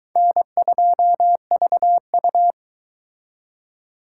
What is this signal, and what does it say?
Morse code, heard as a keyed tone, about 23 wpm, 700 Hz. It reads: N2VU